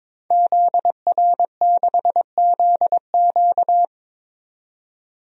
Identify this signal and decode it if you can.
Morse code, heard as a keyed tone, about 22 wpm, 700 Hz. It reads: ZR6ZQ
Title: DXCC-100 Call Signs - 5 Characters - 22wpm